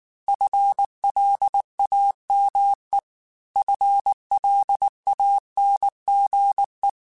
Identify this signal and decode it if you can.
Morse code, heard as a keyed tone, about 19 wpm, 800 Hz. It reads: FLAME FLANGE
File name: FL Runs / FL Runs mp3s / FL Runs @18